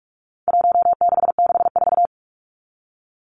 Morse code, heard as a keyed tone, about 44 wpm, 700 Hz. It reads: 1664
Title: Numbers - Contest Style - 4-Digit 45wpm